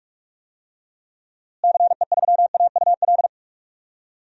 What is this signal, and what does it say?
Morse code, heard as a keyed tone, about 47 wpm, 700 Hz. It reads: CE3AUL
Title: DXCC-100 Call Signs - 6 Characters - Repeated 2x - 45wpm